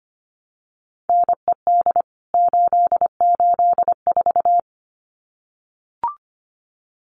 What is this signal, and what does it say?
Morse code, heard as a keyed tone, about 25 wpm, 700 Hz. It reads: NED 884
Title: CWT Exchange 25wpm